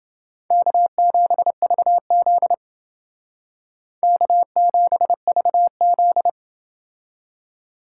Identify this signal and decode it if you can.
Morse code, heard as a keyed tone, about 29 wpm, 700 Hz. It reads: K7VZ K7VZ